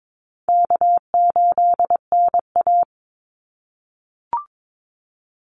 Morse code, heard as a keyed tone, about 22 wpm, 700 Hz. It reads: K8NA
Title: Easy Call Signs - 22wpm